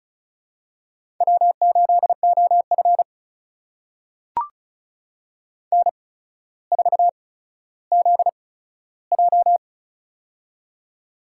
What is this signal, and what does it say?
Morse code, heard as a keyed tone, about 36 wpm, spaced as for 15 wpm, 700 Hz. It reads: W8OF N4ZJ
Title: USA Call Signs - 4 Characters - 8x Character Spacing - 35wpm